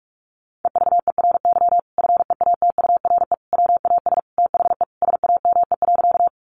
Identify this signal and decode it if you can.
Morse code, heard as a keyed tone, about 44 wpm, 700 Hz. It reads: EVERY FEATURE WAS THE SAME.
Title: Sentences from Top 800 Words 45wpm